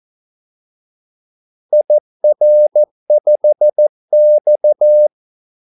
Morse code, heard as a keyed tone, about 14 wpm, 600 Hz. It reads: IR5X
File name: Callsigns-14wpm